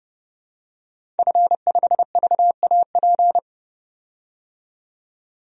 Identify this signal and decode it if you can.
Morse code, heard as a keyed tone, about 30 wpm, 700 Hz. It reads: F5VAP
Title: DXCC-100 Call Signs - 5 Characters - 30wpm